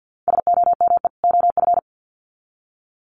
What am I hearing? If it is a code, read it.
Morse code, heard as a keyed tone, about 50 wpm, 700 Hz. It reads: SOME OF
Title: Sets of 2 Words 50wpm